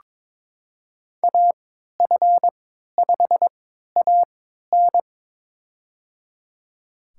Morse code, heard as a keyed tone, about 22 wpm, spaced as for 13 wpm, 700 Hz. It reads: AF5AN
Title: USA Call Signs - 3x Character Spacing - 22wpm